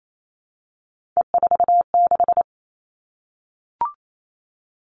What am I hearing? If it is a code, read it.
Morse code, heard as a keyed tone, about 28 wpm, 700 Hz. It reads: E46
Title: All Cut Numbers - 3-Digit 28wpm